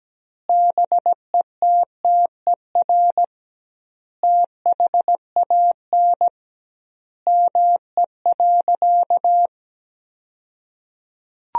Morse code, heard as a keyed tone, about 17 wpm, 700 Hz. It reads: BETTER THAN ME.
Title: 2x Word-Spacing - Sentences from Top 400 Words - 17wpm